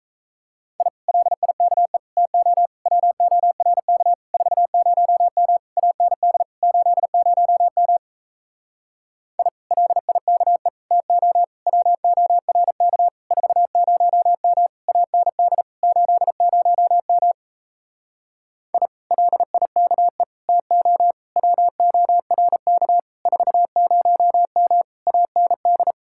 Morse code, heard as a keyed tone, about 37 wpm, 700 Hz. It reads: I LIKE TO WORK 40M AND 80M I LIKE TO WORK 40M AND 80M I LIKE TO WORK 40M AND